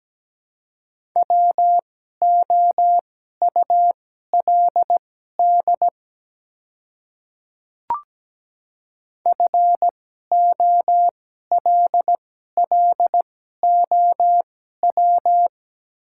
Morse code, heard as a keyed tone, about 17 wpm, spaced as for 12 wpm, 700 Hz. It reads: WOULD FOLLOW
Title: 2x Character Spacing - Top 200 Words - 17wpm